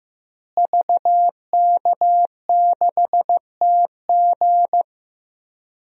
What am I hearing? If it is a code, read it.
Morse code, heard as a keyed tone, about 15 wpm, 700 Hz. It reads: VK6TG
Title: DXCC-100 Call Signs - 5 Characters - Repeated 3x - 15wpm